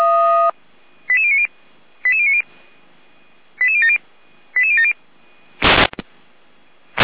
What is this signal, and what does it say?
Five-tone selective call (ZVEI), heard as quick runs of five tones, tones of about 70 ms. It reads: 88090 88090 88070 88070
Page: /455/SDR/ZVEI/88459/2024.05.11/